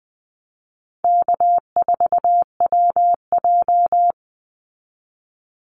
Morse code, heard as a keyed tone, about 20 wpm, 700 Hz. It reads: K4WJ
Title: Easy Call Signs Repeated 3x - 20wpm